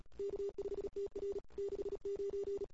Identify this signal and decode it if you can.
Morse code, heard as a keyed tone, about 34 wpm, 410 Hz. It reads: K5TR 69
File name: CW / 2008-ss-cw-k5tr-op-n5rz -23:00